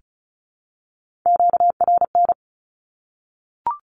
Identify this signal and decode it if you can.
Morse code, heard as a keyed tone, about 34 wpm, 700 Hz. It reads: QRN
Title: Q-Codes Repeated 3x 35wpm